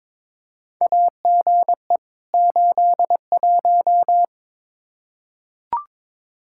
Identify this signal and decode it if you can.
Morse code, heard as a keyed tone, about 22 wpm, 700 Hz. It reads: AGE 81
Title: Practice QSO - Exchange 3 - More Info - 22wpm